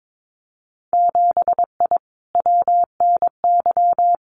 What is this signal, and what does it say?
Morse code, heard as a keyed tone, about 22 wpm, 700 Hz. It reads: 7I WNY
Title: Winter Field Day Exchanges Encore 22wpm